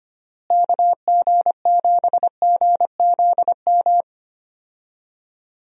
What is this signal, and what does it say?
Morse code, heard as a keyed tone, about 25 wpm, 700 Hz. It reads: KG7GZM
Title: USA Call Signs - 6 Characters - Repeated 2x - 25wpm